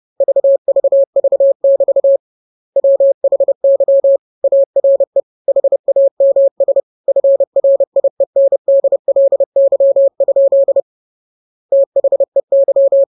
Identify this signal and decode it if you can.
Morse code, heard as a keyed tone, about 30 wpm, 550 Hz. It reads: VVV<BT> WHY ARE HAMS FRIENDLY? THEY
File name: jokes / joke 23 30 0000